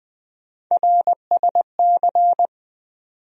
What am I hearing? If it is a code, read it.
Morse code, heard as a keyed tone, about 20 wpm, 700 Hz. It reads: RSC
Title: USA Call Sign Postfixes - 3 Letters 20wpm